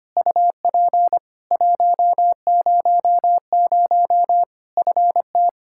Morse code, heard as a keyed tone, about 25 wpm, 700 Hz. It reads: UP 100 FT
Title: Practice QSO - Exchange 2 - Info - 25wpm